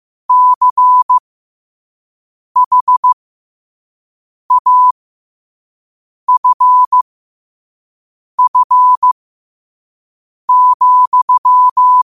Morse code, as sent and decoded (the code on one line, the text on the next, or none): -.-. .... .- ..-. ..-. --..--
CHAFF,